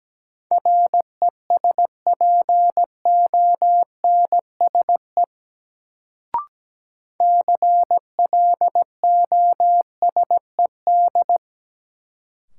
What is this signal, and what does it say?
Morse code, heard as a keyed tone, about 17 wpm, 700 Hz. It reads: RESPONSE CLOSED
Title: Sentences from Top 1400 Words - New Word Drill Alt - 17wpm